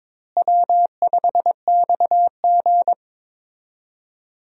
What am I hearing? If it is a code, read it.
Morse code, heard as a keyed tone, about 22 wpm, 700 Hz. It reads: W5XG
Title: USA Call Signs - 4 Characters - Repeated 2x - 22wpm